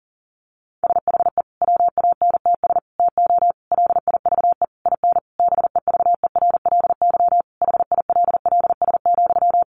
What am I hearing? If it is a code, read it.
Morse code, heard as a keyed tone, about 40 wpm, 700 Hz. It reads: SHE WANTS TO LIVE IN BEVERLY HILLS,
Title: Easy Phrases 40wpm